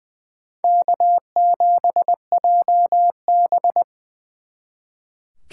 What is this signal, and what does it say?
Morse code, heard as a keyed tone, about 24 wpm, 700 Hz. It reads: K7JB
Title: Speed Racing - USA 1x2 Call Signs 20wpm